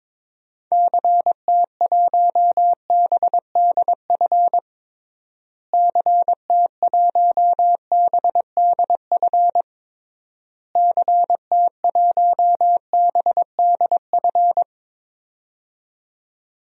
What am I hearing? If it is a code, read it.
Morse code, heard as a keyed tone, about 21 wpm, 700 Hz. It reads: CT1BDF CT1BDF CT1BDF
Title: DXCC-100 Call Signs - 6 Characters - Repeated 3x - 22wpm